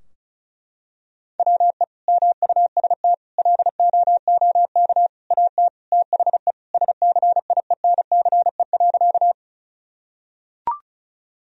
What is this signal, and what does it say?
Morse code, heard as a keyed tone, about 36 wpm, 700 Hz. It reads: WE MUST LOOK AT THE SCIENCE.
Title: Sentences from Top 1000 Words - Review Part 2 - 35wpm